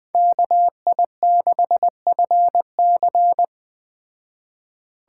Morse code, heard as a keyed tone, about 20 wpm, 700 Hz. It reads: KI6FC